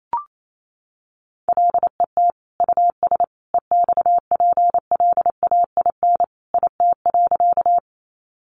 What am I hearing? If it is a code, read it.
Morse code, heard as a keyed tone, about 28 wpm, 700 Hz. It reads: LET US EXPLAIN IT.
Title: Sentences from Top 1100 Words 28wpm